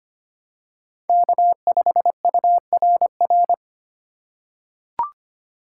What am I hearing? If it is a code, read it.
Morse code, heard as a keyed tone, about 25 wpm, 700 Hz. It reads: K5URR
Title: USA Call Signs - 5 Characters - 3x Character Spacing - 25wpm